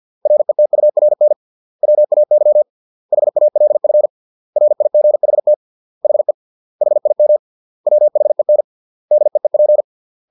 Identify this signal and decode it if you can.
Morse code, heard as a keyed tone, about 50 wpm, 600 Hz. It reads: RETURN WAY HALF RIGHT HE HIM WHEN DEEP